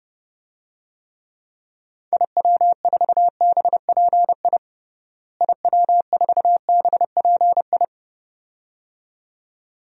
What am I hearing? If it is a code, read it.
Morse code, heard as a keyed tone, about 30 wpm, 700 Hz. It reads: IW4BPI IW4BPI
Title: DXCC-100 Call Signs - 6 Characters - Repeated 2x - 30wpm